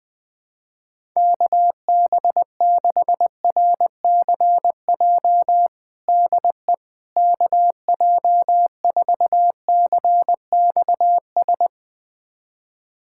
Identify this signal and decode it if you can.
Morse code, heard as a keyed tone, about 21 wpm, 700 Hz. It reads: KB6RCJ DE KJ4CXS
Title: USA Callsign DE Callsign - 6 Characters - 20wpm